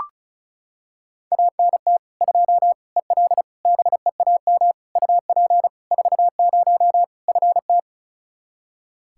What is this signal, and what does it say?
Morse code, heard as a keyed tone, about 36 wpm, 700 Hz. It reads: ANT 2 EL BEAM UP 40 FT
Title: Practice QSO - Exchange 2 - Info - 35wpm